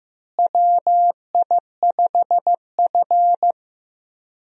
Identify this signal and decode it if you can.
Morse code, heard as a keyed tone, about 15 wpm, 700 Hz. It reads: WI5F
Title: USA Call Signs - 4 Characters - Repeated 3x - 15wpm